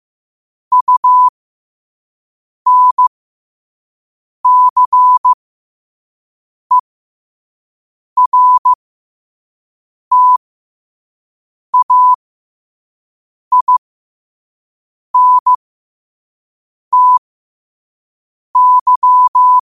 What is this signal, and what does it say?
Morse code, heard as a keyed tone, about 15 wpm, spaced as for 5 wpm, 1000 Hz. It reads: UNCERTAINTY
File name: qotdfiles / 5wpm